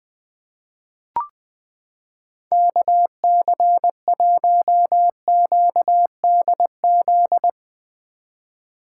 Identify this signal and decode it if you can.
Morse code, heard as a keyed tone, about 20 wpm, 700 Hz. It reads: KC1QDZ
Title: 2023 POTA Hunter Call Signs - 20wpm